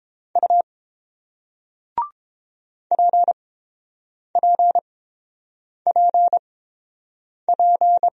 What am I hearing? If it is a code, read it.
Morse code, heard as a keyed tone, about 29 wpm, spaced as for 7 wpm, 700 Hz. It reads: U PPPP